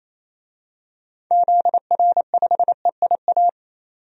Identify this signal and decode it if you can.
Morse code, heard as a keyed tone, about 28 wpm, 700 Hz. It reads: ZR5EIA
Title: DXCC-100 Call Signs - 6 Characters - 28wpm